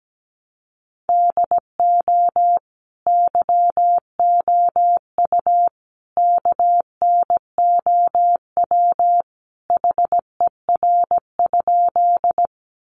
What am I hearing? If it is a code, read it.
Morse code, heard as a keyed tone, about 17 wpm, 700 Hz. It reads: DO YOU KNOW HER?
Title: Sentences from Top 100 Words 17wpm